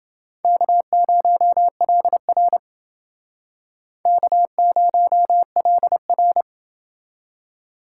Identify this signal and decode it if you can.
Morse code, heard as a keyed tone, about 29 wpm, 700 Hz. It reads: K0LR K0LR